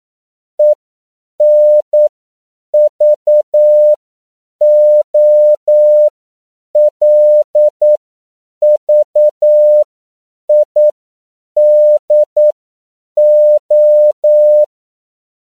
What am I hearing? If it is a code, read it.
Morse code, heard as a keyed tone, about 9 wpm, spaced as for 7 wpm, 600 Hz. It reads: ENVOLVIDO
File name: CW / 10PPM